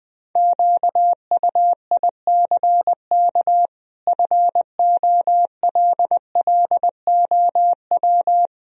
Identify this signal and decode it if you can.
Morse code, heard as a keyed tone, about 20 wpm, 700 Hz. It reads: QUICK FOLLOW
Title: Sentences from Top 1000 Words - Review Part 1 - 20wpm